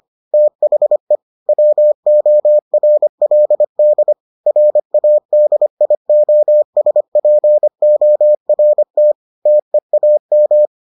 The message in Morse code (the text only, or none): THE WORLD RADIOSPORT TEAM